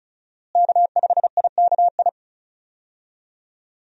Morse code, heard as a keyed tone, about 36 wpm, 700 Hz. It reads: K5IKI
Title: USA Call Signs - 5 Characters - Repeated 3x - 35wpm